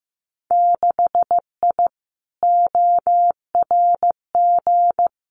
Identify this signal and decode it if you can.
Morse code, heard as a keyed tone, about 15 wpm, 700 Hz. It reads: 6I ORG
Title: Winter Field Day Exchanges Encore 15wpm